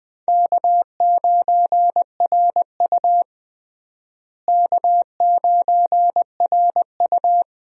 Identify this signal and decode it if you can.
Morse code, heard as a keyed tone, about 20 wpm, 700 Hz. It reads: K9RU K9RU
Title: USA Call Signs - 4 Characters - Repeated 2x - 20wpm